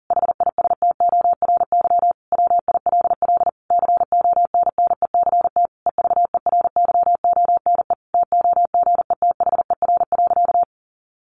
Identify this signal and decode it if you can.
Morse code, heard as a keyed tone, about 37 wpm, 700 Hz. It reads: HISTORY WILL CONNECT EVERYONE TOGETHER.
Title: Sentences from Top 1500 Words 40wpm